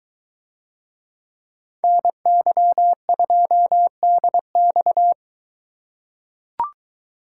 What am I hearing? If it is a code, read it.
Morse code, heard as a keyed tone, about 22 wpm, 700 Hz. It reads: NY2DX